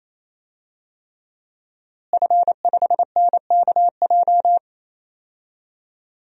Morse code, heard as a keyed tone, about 28 wpm, 700 Hz. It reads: F5NKJ
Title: DXCC-100 Call Signs - 5 Characters - Repeated 3x - 28wpm